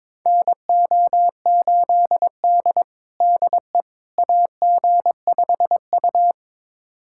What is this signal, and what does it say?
Morse code, heard as a keyed tone, about 22 wpm, 700 Hz. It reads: NO8D DE AG5U
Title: USA Callsign DE Callsign - 4 Characters - 22wpm